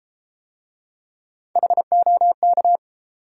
Morse code, heard as a keyed tone, about 30 wpm, 700 Hz. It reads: HOK